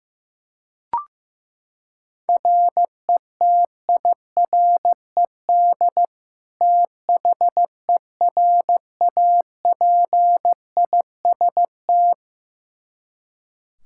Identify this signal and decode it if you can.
Morse code, heard as a keyed tone, about 15 wpm, 700 Hz. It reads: RETIRED THERAPIST
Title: Practice QSO - Exchange 3 - More Info - 15wpm